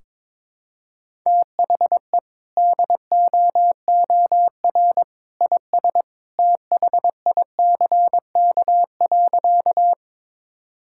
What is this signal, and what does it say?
Morse code, heard as a keyed tone, about 22 wpm, 700 Hz. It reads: THE DOOR IS THICK.
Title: Sentences from Top 1600 Words 22wpm